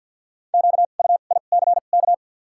Morse code, heard as a keyed tone, about 53 wpm, spaced as for 37 wpm, 700 Hz. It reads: QUICK
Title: 2x Character Spacing - Top 500 Words - 50wpm